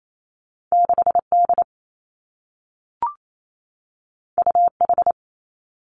Morse code, heard as a keyed tone, about 28 wpm, 700 Hz. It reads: 6 D U H
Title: Two Challenging Letter-Number Combinations 28wpm